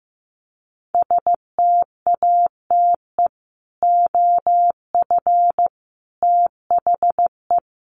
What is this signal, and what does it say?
Morse code, heard as a keyed tone, about 15 wpm, 700 Hz. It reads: STATE OF THE
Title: Sets of 5 Words Encore 15wpm